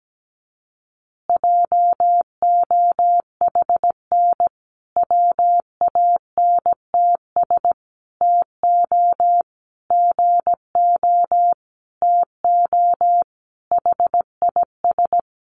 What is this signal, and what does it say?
Morse code, heard as a keyed tone, about 17 wpm, 700 Hz. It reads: JOHN WANTS TO GO TO HIS
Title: Additional Easy Phrases 17wpm